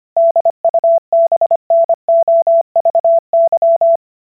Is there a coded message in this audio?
Yes: Morse code from 0.2 to 4.0 s, heard as a keyed tone, about 25 wpm, 670 Hz. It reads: DUBNOVY